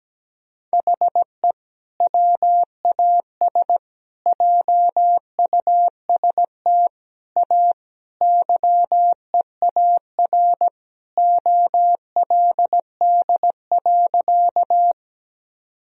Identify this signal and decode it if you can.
Morse code, heard as a keyed tone, about 17 wpm, 700 Hz. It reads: HE WAS JUST A YEAR OLD.
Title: Sentences from Top 1000 Words - Review Part 1 - 17wpm